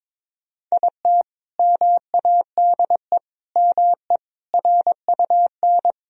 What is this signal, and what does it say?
Morse code, heard as a keyed tone, about 22 wpm, 700 Hz. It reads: IT MADE ME RUN
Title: Sentences from Top 1000 Words - Review Part 1 - 22wpm